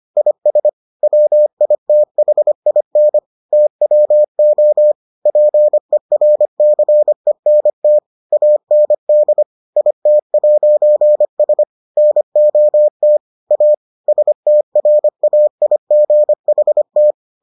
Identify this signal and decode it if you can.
Morse code, heard as a keyed tone, about 25 wpm, 600 Hz. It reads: IS WITHIN TWO PERCENT AND IT'S NOT A STRAIGHT